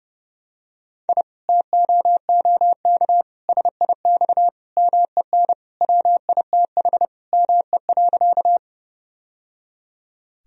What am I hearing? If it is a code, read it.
Morse code, heard as a keyed tone, about 30 wpm, 700 Hz. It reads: I TOOK SIX MEN WITH ME.
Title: Sentences from Top 1000 Words - Review Part 2 - 30wpm